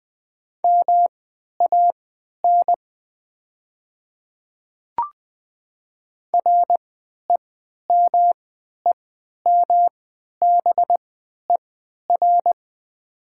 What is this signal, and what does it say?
Morse code, heard as a keyed tone, about 20 wpm, spaced as for 11 wpm, 700 Hz. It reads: MAN REMEMBER
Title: 3x Character Spacing - Top 500 Words - 20wpm